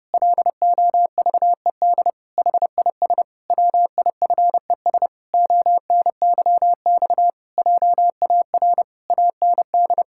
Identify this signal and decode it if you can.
Morse code, heard as a keyed tone, about 30 wpm, 700 Hz. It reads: LOVED HIS WIFES ONYX JAR AND
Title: Pangram Challenge v2 - 30wpm